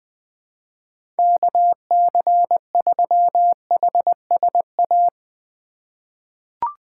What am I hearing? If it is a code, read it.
Morse code, heard as a keyed tone, about 21 wpm, 700 Hz. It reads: KC3HSA
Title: USA Call Signs - 6 Characters - 4x Character Spacing - 20wpm